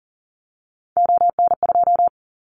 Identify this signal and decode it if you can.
Morse code, heard as a keyed tone, about 40 wpm, 700 Hz. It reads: ON2
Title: Callsign Prefixes for Top 25 DXCC - Repeated 3x - 40wpm